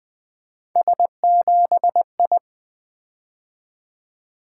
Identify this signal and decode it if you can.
Morse code, heard as a keyed tone, about 20 wpm, 700 Hz. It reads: S7I
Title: Three Letter-Number Combinations - 20wpm